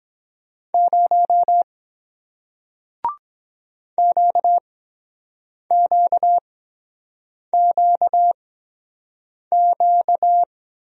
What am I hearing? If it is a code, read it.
Morse code, heard as a keyed tone, about 23 wpm, spaced as for 7 wpm, 700 Hz. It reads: 0 QQQQ